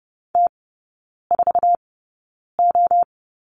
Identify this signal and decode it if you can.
Morse code, heard as a keyed tone, about 30 wpm, spaced as for 9 wpm, 700 Hz. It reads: T4O
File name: Lesson 19e - Letter w - Three Characters 30wpm